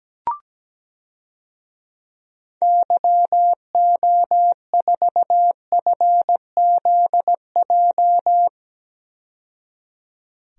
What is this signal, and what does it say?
Morse code, heard as a keyed tone, about 17 wpm, 700 Hz. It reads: YO4FZJ
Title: DXCC-100 Call Signs - 6 Characters - 17wpm